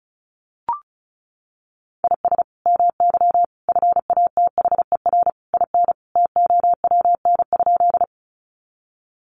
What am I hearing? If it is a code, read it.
Morse code, heard as a keyed tone, about 34 wpm, 700 Hz. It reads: IS MY FATHER IN TOWN?
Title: Sentences from Top 500 Words 35wpm